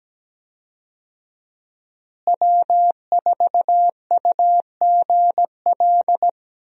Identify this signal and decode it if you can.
Morse code, heard as a keyed tone, about 17 wpm, 700 Hz. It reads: W4UGL
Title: USA Call Signs - 5 Characters - Repeated 3x - 17wpm